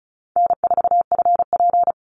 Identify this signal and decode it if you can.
Morse code, heard as a keyed tone, about 34 wpm, 700 Hz. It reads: N4FP
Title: Call Signs Repeated 2x - 35wpm